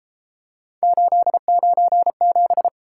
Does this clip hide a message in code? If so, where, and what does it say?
Yes: Morse code from 0.8 to 2.7 s, heard as a keyed tone, about 33 wpm, 700 Hz. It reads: 897